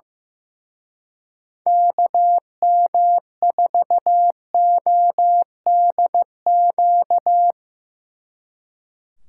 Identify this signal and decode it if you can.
Morse code, heard as a keyed tone, about 16 wpm, 700 Hz. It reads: KM4ODQ
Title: Speed Racing - USA 2x3 Call Signs 15wpm